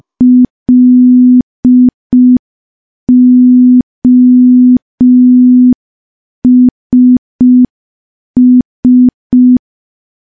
Morse code, heard as a keyed tone, about 5 wpm, 250 Hz. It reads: LOSS
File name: morse / morse-training-05WPM